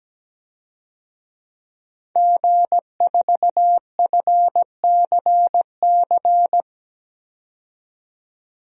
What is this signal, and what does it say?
Morse code, heard as a keyed tone, about 17 wpm, 700 Hz. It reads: G4FCC
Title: DXCC-100 Call Signs - 5 Characters - 17wpm